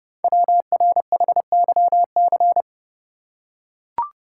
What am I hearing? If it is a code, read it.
Morse code, heard as a keyed tone, about 30 wpm, 690 Hz. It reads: WRHYC
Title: Code Groups - 5 Letters - 30wpm